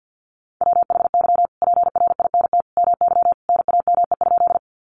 Morse code, heard as a keyed tone, about 51 wpm, 700 Hz. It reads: WHY PRINT MY NAME?
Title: Sentences from Top 1400 Words 50wpm